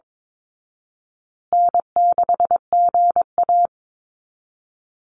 Morse code, heard as a keyed tone, about 22 wpm, 700 Hz. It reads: N6GA
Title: Easy Call Signs Repeated 3x - 22wpm